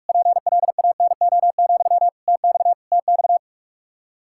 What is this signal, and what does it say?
Morse code, heard as a keyed tone, about 46 wpm, 700 Hz. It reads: PLANO, TX TX
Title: Practice QSO - Exchange 1 - Beginning - 45wpm